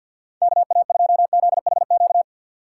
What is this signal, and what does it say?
Morse code, heard as a keyed tone, about 47 wpm, 700 Hz. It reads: KA2ZHQ
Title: USA Call Signs - 6 Characters - 4x Character Spacing - 50wpm